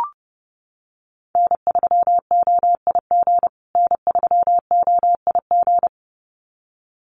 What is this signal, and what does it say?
Morse code, heard as a keyed tone, about 30 wpm, 700 Hz. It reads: N3OIG N3OIG
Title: USA Call Signs Repeated 2x 30wpm